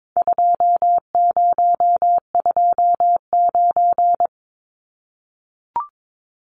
Morse code, heard as a keyed tone, about 22 wpm, 700 Hz. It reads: 2029
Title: Numbers - 4-Digit 22wpm